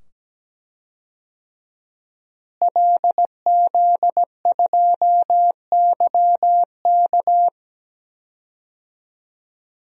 Morse code, heard as a keyed tone, about 17 wpm, 700 Hz. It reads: LZ2YK
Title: DXCC-100 Call Signs - 5 Characters - Repeated 2x - 17wpm